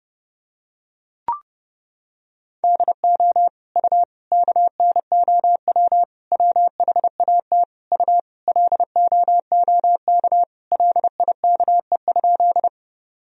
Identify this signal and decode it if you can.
Morse code, heard as a keyed tone, about 30 wpm, 700 Hz. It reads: DO U KNOW WHAT U LOOK LIKE?
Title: Sentences from Top 100 Words and Abbreviations 30wpm